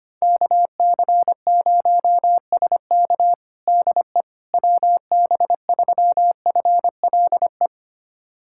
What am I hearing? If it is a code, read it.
Morse code, heard as a keyed tone, about 25 wpm, 700 Hz. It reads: KC0SK DE WB3FLE
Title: USA Callsign DE Callsign 25wpm